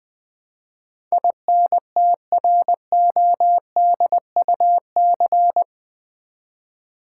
Morse code, heard as a keyed tone, about 20 wpm, 700 Hz. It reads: INTRODUC
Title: Build A Word - 4 Letters At A Time - 12-Letter Words - 20wpm